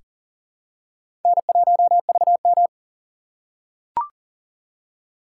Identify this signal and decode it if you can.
Morse code, heard as a keyed tone, about 40 wpm, 700 Hz. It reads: N1VM
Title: USA Call Signs - 3x Character Spacing - 40wpm